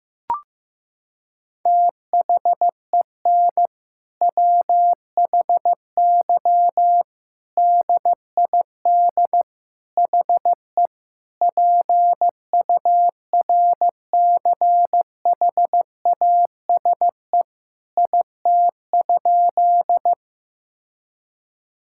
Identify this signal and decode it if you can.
Morse code, heard as a keyed tone, about 15 wpm, 700 Hz. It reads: THEN WHY DID HE PURCHASE IT?
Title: Sentences from Top 1000 Words - Review Part 5 - 15wpm